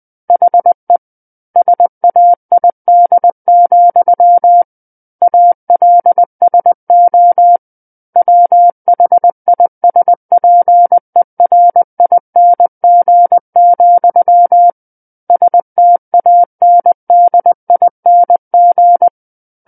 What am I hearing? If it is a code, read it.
Morse code, heard as a keyed tone, about 20 wpm, 700 Hz. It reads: HE SAID, ALSO WHISPERING, STANDING
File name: 20 WPM